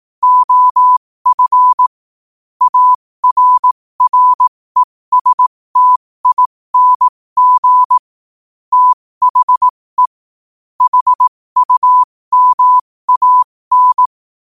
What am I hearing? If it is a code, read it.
Morse code, heard as a keyed tone, about 18 wpm, spaced as for 15 wpm, 1000 Hz. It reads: OF ARRESTING THE HUMAN
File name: qotdfiles / 15wpm